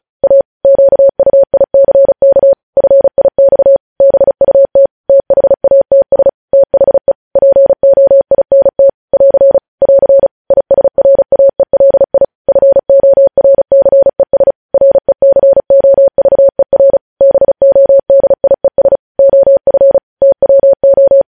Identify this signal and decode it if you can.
Morse code, heard as a keyed tone, about 36 wpm, 550 Hz. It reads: A QUICK FIX BUT THATS THE POINT <AR> <AR> ISRAELI FORCES RECOVER BODIES OF TWO
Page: news headlines in morse code at 35 wpm